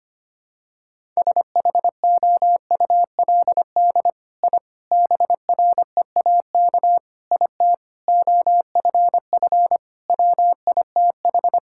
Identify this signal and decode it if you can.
Morse code, heard as a keyed tone, about 25 wpm, 700 Hz. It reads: SHOULD I BREAK IT OFF WITH
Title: Sentences from Top 1000 Words - Review Part 6 - 25wpm